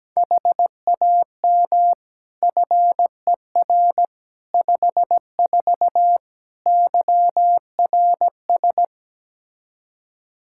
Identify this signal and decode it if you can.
Morse code, heard as a keyed tone, about 17 wpm, 700 Hz. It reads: HAM FER 54 YRS
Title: Practice QSO - Exchange 3 - More Info - 17wpm